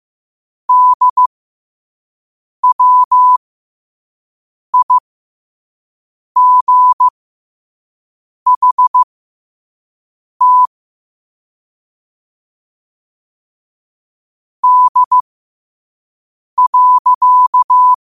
Morse code, heard as a keyed tone, about 15 wpm, spaced as for 5 wpm, 1000 Hz. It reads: DWIGHT D.
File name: qotdfiles / 5wpm